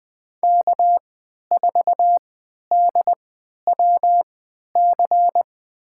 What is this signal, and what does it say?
Morse code, heard as a keyed tone, about 20 wpm, spaced as for 11 wpm, 700 Hz. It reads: K4DWC
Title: USA Call Signs - 5 Characters - 3x Character Spacing - 20wpm